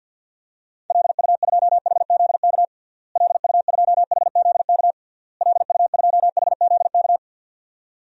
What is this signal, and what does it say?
Morse code, heard as a keyed tone, about 52 wpm, 700 Hz. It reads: LU2HZK LU2HZK LU2HZK
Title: DXCC-100 Call Signs - 6 Characters - Repeated 3x - 50wpm